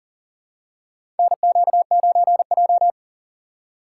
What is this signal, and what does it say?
Morse code, heard as a keyed tone, about 40 wpm, 700 Hz. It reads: NQ9J
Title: USA Call Signs - 4 Characters - 3x Character Spacing - 40wpm